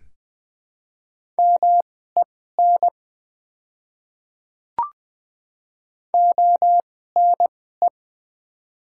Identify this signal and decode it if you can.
Morse code, heard as a keyed tone, about 20 wpm, spaced as for 14 wpm, 700 Hz. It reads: MEN ONE
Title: 2x Character Spacing - Top 200 Words - 20wpm